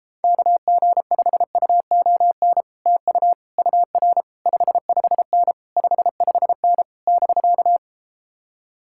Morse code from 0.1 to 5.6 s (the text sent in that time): KG5UON TU UR 55N